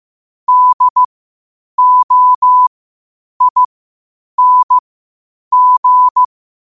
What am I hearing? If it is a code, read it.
Morse code, heard as a keyed tone, about 15 wpm, spaced as for 8 wpm, 1000 Hz. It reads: DOING
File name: qotdfiles / 8wpm